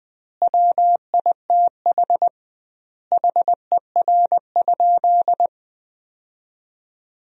Morse code, heard as a keyed tone, about 21 wpm, 700 Hz. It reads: WITH HER?
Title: 2x Word-Spacing - Sentences from Top 400 Words - 20wpm